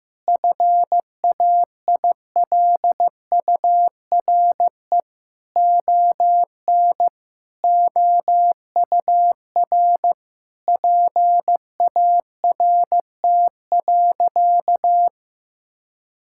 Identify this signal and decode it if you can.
Morse code, heard as a keyed tone, about 15 wpm, 700 Hz. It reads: FAILURE ON OUR PART.
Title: Sentences from Top 1000 Words - Review Part 6 - 15wpm